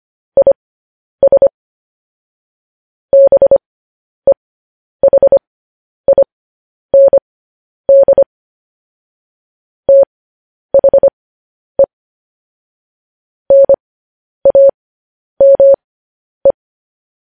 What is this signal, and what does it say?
Morse code, heard as a keyed tone, about 25 wpm, spaced as for 10 wpm, 550 Hz. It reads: IS BEHIND THE NAME
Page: news-headlines-in-morse-code-at-10-wpm